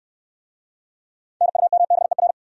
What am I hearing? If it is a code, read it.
Morse code, heard as a keyed tone, about 67 wpm, 700 Hz. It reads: NUMBER